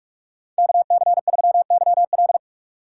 Morse code, heard as a keyed tone, about 44 wpm, 700 Hz. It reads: KC3YL